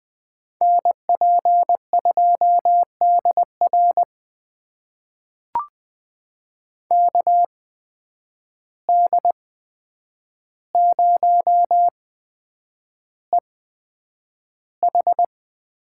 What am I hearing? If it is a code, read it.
Morse code, heard as a keyed tone, about 20 wpm, spaced as for 9 wpm, 700 Hz. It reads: NP2DR KD0EH
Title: USA Call Signs - 5 Characters - 8x Character Spacing - 20wpm